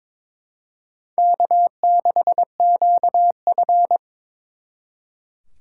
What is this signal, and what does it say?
Morse code, heard as a keyed tone, about 26 wpm, 700 Hz. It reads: K6QF
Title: Speed Racing - USA 1x2 Call Signs 22wpm